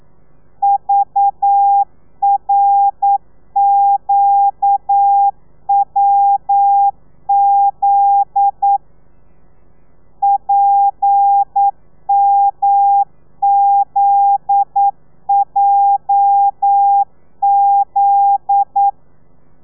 Morse code, heard as a keyed tone, about 9 wpm, 800 Hz. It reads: VRQWZ PMZJZ
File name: telegrafia / i2tfj / Lez9 40